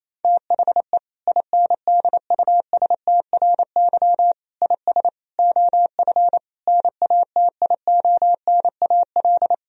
Morse code, heard as a keyed tone, about 27 wpm, 700 Hz. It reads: THE INDUSTRY IS OF NATIONAL
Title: Sentences from Top 1000 Words - Review Part 4 - 28wpm